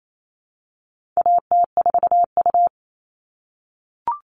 Morse code, heard as a keyed tone, about 28 wpm, 700 Hz. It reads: AT4U
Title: All Cut Numbers - 4-Digit 28wpm